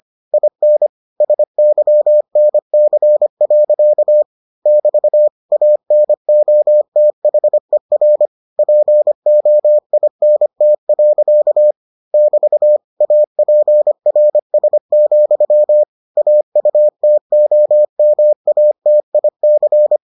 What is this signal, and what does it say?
Morse code, heard as a keyed tone, about 25 wpm, 600 Hz. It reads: IN SYNC. <BT> ANOTHER POINT. <BT> APRS, AUTOMATIC